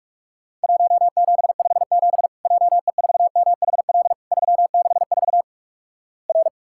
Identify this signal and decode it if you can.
Morse code, heard as a keyed tone, about 46 wpm, 700 Hz. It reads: 1757 JE4MHL 364 R
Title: Worked All Europe DX Contest - QTC Exchange - 45wpm